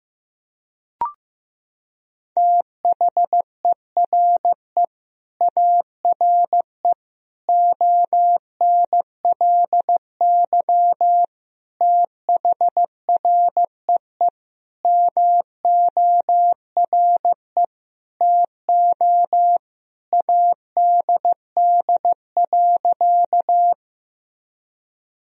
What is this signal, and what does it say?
Morse code, heard as a keyed tone, about 16 wpm, 700 Hz. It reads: THERE ARE ONLY THREE MORE TO ADD.
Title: Sentences from Top 1000 Words - Review Part 1 - 15wpm